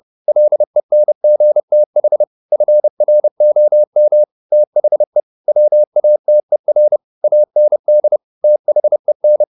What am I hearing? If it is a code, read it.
Morse code, heard as a keyed tone, about 30 wpm, 600 Hz. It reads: LENGTH FROM THE WATER AND THEN
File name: the old man and the sea / the old man and the sea 1 30wpm